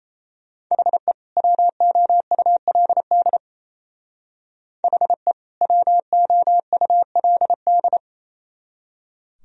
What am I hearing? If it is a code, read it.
Morse code, heard as a keyed tone, about 34 wpm, 700 Hz. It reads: HE WOULD HE WOULD
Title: Speed-Racing - Sets of 2 Words 28wpm